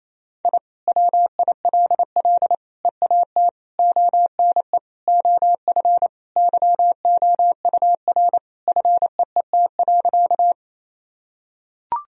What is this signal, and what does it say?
Morse code, heard as a keyed tone, about 28 wpm, 700 Hz. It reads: I WILL EAT ONE OF YOUR FEET.
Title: Sentences from Top 1000 Words - Review Part 1 - 28wpm